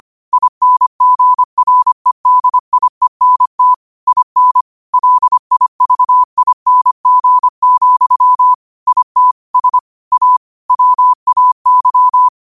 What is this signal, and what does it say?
Morse code, heard as a keyed tone, about 25 wpm, 1000 Hz. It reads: INGREDIENT IN LIVING, IT S A WAY
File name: qotdfiles / 25wpm